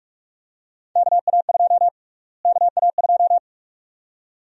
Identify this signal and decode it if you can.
Morse code, heard as a keyed tone, about 46 wpm, 700 Hz. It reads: KA2 KA2
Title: USA Call Sign Prefixes - Repeated 2x - 45wpm